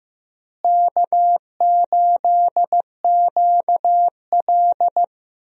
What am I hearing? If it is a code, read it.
Morse code, heard as a keyed tone, about 15 wpm, 700 Hz. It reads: K8QL
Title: USA Call Signs - 4 Characters - Repeated 3x - 15wpm